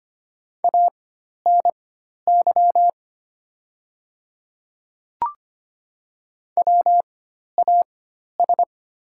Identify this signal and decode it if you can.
Morse code, heard as a keyed tone, about 25 wpm, spaced as for 12 wpm, 700 Hz. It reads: ANY WAS
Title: 4x Character Spacing - Top 200 Words - 25wpm